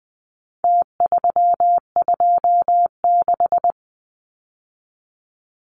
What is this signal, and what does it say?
Morse code, heard as a keyed tone, about 20 wpm, 700 Hz. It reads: T326
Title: Numbers - Contest Style - 4-Digit 20wpm